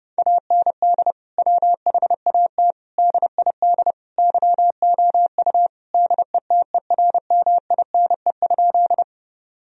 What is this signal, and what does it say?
Morse code, heard as a keyed tone, about 30 wpm, 700 Hz. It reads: AND WHAT DID YOU DETERMINE?
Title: Sentences from Top 1000 Words - Review Part 4 - 30wpm